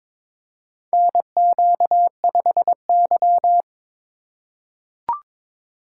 Morse code, heard as a keyed tone, about 22 wpm, 700 Hz. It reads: NQ5Y
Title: USA Call Signs - 4 Characters - 3x Character Spacing - 22wpm